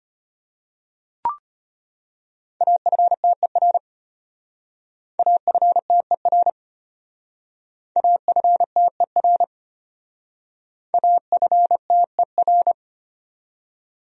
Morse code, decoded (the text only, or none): AFTER AFTER AFTER AFTER